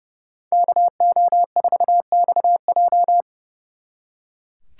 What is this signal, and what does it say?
Morse code, heard as a keyed tone, about 37 wpm, 700 Hz. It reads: KO4XJ
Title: Speed Racing - USA 2x2 Call Signs 30wpm